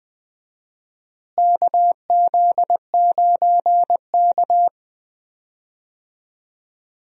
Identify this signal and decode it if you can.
Morse code, heard as a keyed tone, about 20 wpm, 700 Hz. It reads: KZ9K
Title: USA Call Signs - 4 Characters - Repeated 3x - 20wpm